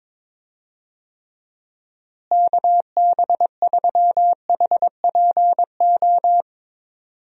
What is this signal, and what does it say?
Morse code, heard as a keyed tone, about 21 wpm, 700 Hz. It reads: KB3HPO
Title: USA Call Signs - 6 Characters - Repeated 3x - 22wpm